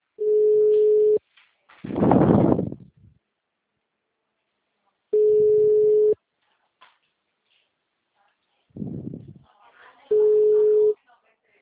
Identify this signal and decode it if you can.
Ringback tone (Europe, 425 Hz), with speech.
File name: out- -2004- - - .13819